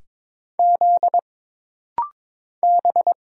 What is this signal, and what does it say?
Morse code, heard as a keyed tone, about 22 wpm, 700 Hz. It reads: Z B
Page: This rapid-fire timing is geared to push your ICR (Instant Character Recognition) skills.